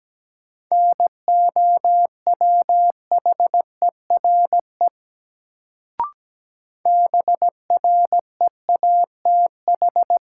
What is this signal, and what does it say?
Morse code, heard as a keyed tone, about 17 wpm, 700 Hz. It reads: NOWHERE BREATH
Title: Sets of 5 Words Encore - New Word Drill - 17wpm